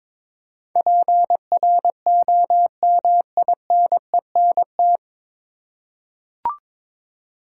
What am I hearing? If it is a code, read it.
Morse code, heard as a keyed tone, about 22 wpm, 700 Hz. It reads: PROMINENT
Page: Morse Code Ninja practice 22wpm